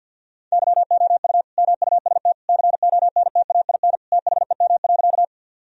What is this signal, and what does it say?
Morse code, heard as a keyed tone, about 50 wpm, 700 Hz. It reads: YOU MUST CONTAIN THEM.
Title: Sentences from Top 1000 Words - Review Part 2 - 50wpm